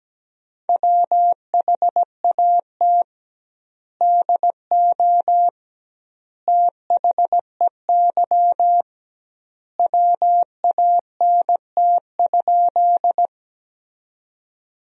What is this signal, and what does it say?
Morse code, heard as a keyed tone, about 17 wpm, 700 Hz. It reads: WHAT DO THEY WANT?
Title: 2x Word-Spacing - Sentences from Top 200 Words - 17wpm